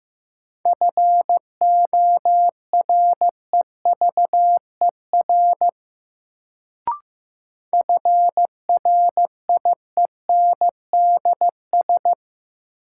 Morse code, heard as a keyed tone, about 15 wpm, 700 Hz. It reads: FOREVER FRIENDS
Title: Sentences from Top 1600 Words - New Word Drill Alt - 15wpm